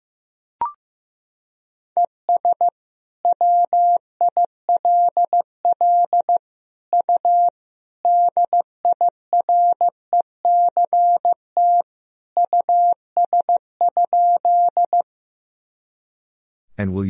Morse code, heard as a keyed tone, about 16 wpm, 700 Hz. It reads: ES WILL U DIRECT US?
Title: Sentences from Top 400 Words and Abbreviations 15wpm